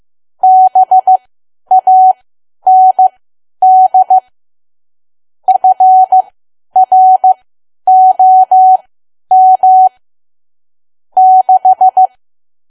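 Morse code, heard as a keyed tone, about 15 wpm, spaced as for 10 wpm, 750 Hz. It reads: BAND FROM 6
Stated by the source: Morse learning practice at 10 wpm.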